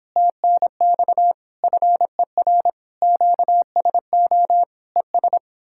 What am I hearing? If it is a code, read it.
Morse code, heard as a keyed tone, about 29 wpm, 700 Hz. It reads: TNX FER QSO ES